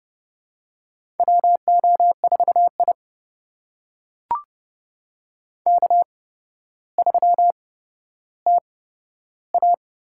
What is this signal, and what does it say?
Morse code, heard as a keyed tone, about 30 wpm, spaced as for 13 wpm, 700 Hz. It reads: WO4I K3TA
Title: USA Call Signs - 4 Characters - 8x Character Spacing - 30wpm